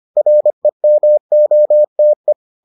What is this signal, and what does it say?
Morse code, heard as a keyed tone, about 25 wpm, 600 Hz. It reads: REMOTE